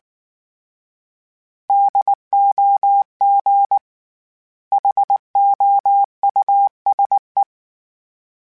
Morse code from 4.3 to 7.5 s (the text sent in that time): HOUSE